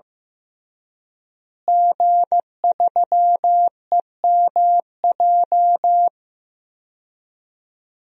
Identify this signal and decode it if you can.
Morse code, heard as a keyed tone, about 15 wpm, 700 Hz. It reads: G3EMJ
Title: DXCC-100 Call Signs - 5 Characters - Repeated 3x - 15wpm